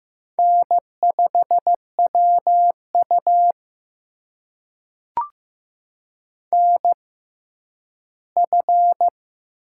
Morse code, heard as a keyed tone, about 15 wpm, spaced as for 10 wpm, 700 Hz. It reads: N5WU NF
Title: USA Call Signs - 4 Characters - 6x Character Spacing - 15wpm